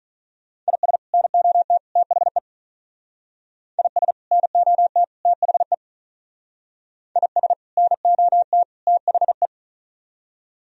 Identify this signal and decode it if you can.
Morse code, heard as a keyed tone, about 44 wpm, 700 Hz. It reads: IS NOT THE IS NOT THE IS NOT THE